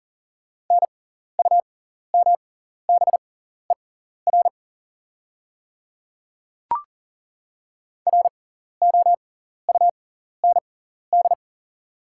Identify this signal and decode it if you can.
Morse code, heard as a keyed tone, about 40 wpm, spaced as for 14 wpm, 700 Hz. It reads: NUMBER ROUND
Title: 6x Character Spacing - Top 200 Words - 40wpm